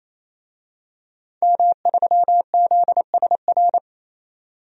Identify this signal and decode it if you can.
Morse code, heard as a keyed tone, about 28 wpm, 700 Hz. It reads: M3ZSR
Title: DXCC-100 Call Signs - 5 Characters - 28wpm